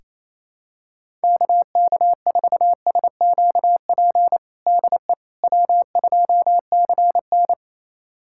Morse code, heard as a keyed tone, about 28 wpm, 700 Hz. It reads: KK4SQP DE W2CN
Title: USA Callsign DE Callsign 28wpm